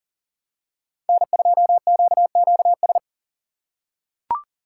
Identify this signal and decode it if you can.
Morse code, heard as a keyed tone, about 40 wpm, 700 Hz. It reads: N2QQS
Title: USA Call Signs - 5 Characters - 6x Character Spacing - 40wpm